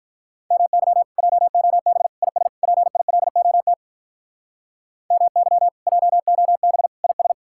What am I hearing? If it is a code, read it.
Morse code, heard as a keyed tone, about 54 wpm, 700 Hz. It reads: MY JOB IS PILOT MY JOB IS